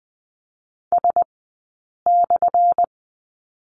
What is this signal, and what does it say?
Morse code, heard as a keyed tone, about 20 wpm, spaced as for 8 wpm, 690 Hz. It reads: S/
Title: 197 Lesson 32d - Letter k - Two Characters 20wpm